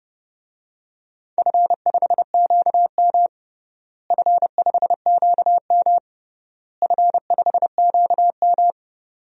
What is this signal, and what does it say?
Morse code, heard as a keyed tone, about 30 wpm, 700 Hz. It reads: F5QM F5QM F5QM
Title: DXCC-100 Call Signs - 4 Characters - Repeated 3x - 30wpm